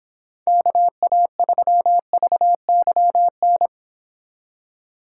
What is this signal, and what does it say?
Morse code, heard as a keyed tone, about 29 wpm, 700 Hz. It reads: KA3VYN